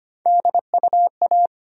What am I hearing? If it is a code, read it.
Morse code, heard as a keyed tone, about 25 wpm, 700 Hz. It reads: DUA
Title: USA Call Sign Postfixes - 3 Letters 25wpm